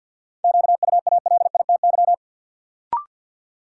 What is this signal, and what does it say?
Morse code, heard as a keyed tone, about 52 wpm, 700 Hz. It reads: QUALITY
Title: Sentences from Top 1500 Words - New Word Drill Alt - 50wpm